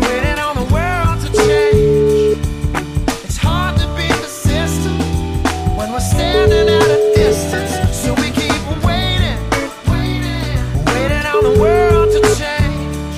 Ringback tone (Europe, 425 Hz), with noise.